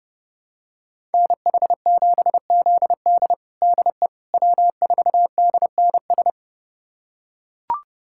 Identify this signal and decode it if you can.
Morse code, heard as a keyed tone, about 30 wpm, 700 Hz. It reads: NH7ZD DE W4DNS
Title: USA Callsign DE Callsign - 5 Characters - 30wpm